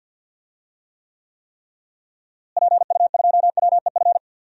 Morse code, heard as a keyed tone, about 52 wpm, 700 Hz. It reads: PU2WEF
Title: DXCC-100 Call Signs - 6 Characters - Repeated 3x - 50wpm